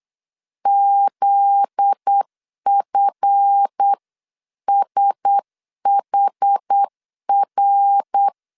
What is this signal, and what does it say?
Morse code, heard as a keyed tone, about 8 wpm, 800 Hz. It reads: ZFSHR
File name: CW / DET / 1 9 1500C S A 8wpm 5637